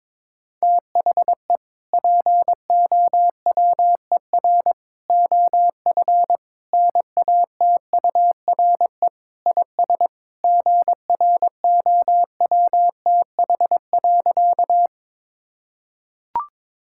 Morse code, heard as a keyed tone, about 22 wpm, 700 Hz. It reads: THE POWER OF NATURE IS GROWTH.
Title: Sentences from Top 1000 Words - Review Part 5 - 22wpm